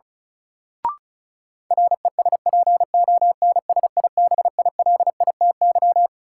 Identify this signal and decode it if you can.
Morse code, heard as a keyed tone, about 36 wpm, 700 Hz. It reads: RESPONSIBILITY
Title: Sentences from Top 1200 Words - New Word Drill - 35wpm